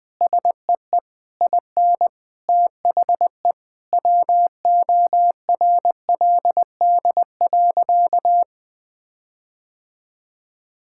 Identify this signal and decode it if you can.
Morse code, heard as a keyed tone, about 20 wpm, 700 Hz. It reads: SEE IN THE WORLD.
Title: Sentences with 9-Word Phrases - 20wpm